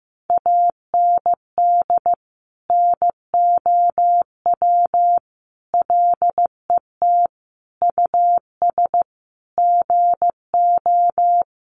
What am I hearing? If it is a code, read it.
Morse code, heard as a keyed tone, about 15 wpm, 700 Hz. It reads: AND NOW LET US GO
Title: Sentences from Top 1500 Words 15wpm